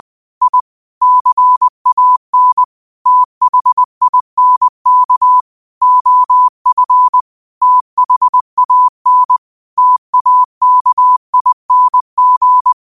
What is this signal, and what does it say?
Morse code, heard as a keyed tone, about 20 wpm, 1000 Hz. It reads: I CAN THINK OF THAN TAKING
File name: qotdfiles / 20wpm